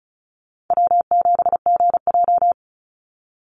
Morse code, heard as a keyed tone, about 36 wpm, 700 Hz. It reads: W7GJ
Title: NA SOTA Chaser Callsigns 35wpm